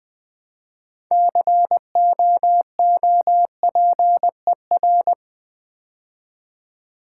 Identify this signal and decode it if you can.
Morse code, heard as a keyed tone, about 20 wpm, 700 Hz. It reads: COOPER
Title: Build A Word - 3 Letters At A Time - 9-Letter Words - 20wpm